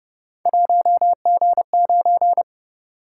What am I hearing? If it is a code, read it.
Morse code, heard as a keyed tone, about 28 wpm, 700 Hz. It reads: 1G9